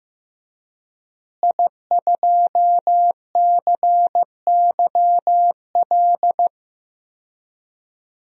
Morse code, heard as a keyed tone, about 15 wpm, 700 Hz. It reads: I2CYL
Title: DXCC-100 Call Signs - 5 Characters - Repeated 3x - 15wpm